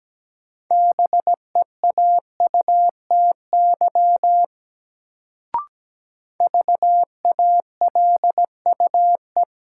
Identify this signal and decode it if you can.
Morse code, heard as a keyed tone, about 17 wpm, 700 Hz. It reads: BEAUTY VALUE
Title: Sentences from Top 600 Words - New Word Drill - 17wpm